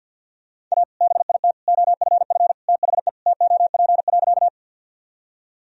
Morse code, heard as a keyed tone, about 53 wpm, 700 Hz. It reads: A BIT OFF THE TOP.
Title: Sentences from Top 1000 Words - Review Part 2 - 50wpm